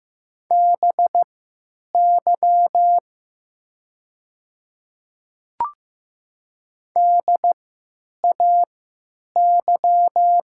Morse code, heard as a keyed tone, about 15 wpm, spaced as for 9 wpm, 700 Hz. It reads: BY DAY
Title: 3x Character Spacing - Top 100 Words - 15wpm